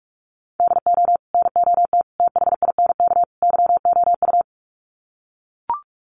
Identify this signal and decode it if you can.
Morse code, heard as a keyed tone, about 44 wpm, 700 Hz. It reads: DO NOT THINK YOU
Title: Sets of 4 Words 45wpm